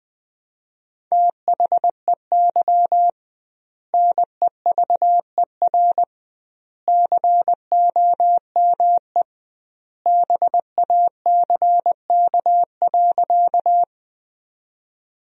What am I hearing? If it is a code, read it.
Morse code, heard as a keyed tone, about 21 wpm, 700 Hz. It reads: THEY NEVER COME BACK.
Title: 2x Word-Spacing - Sentences from Top 300 Words - 20wpm